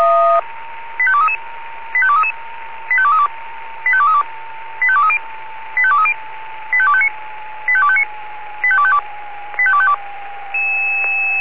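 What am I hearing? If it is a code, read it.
Five-tone selective call (ZVEI), heard as quick runs of five tones, tones of about 70 ms. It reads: 85233 85233 85232 85232 85230 85230 85259 85259 85252 85252
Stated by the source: ZVEI five-tone selective call received by SDR.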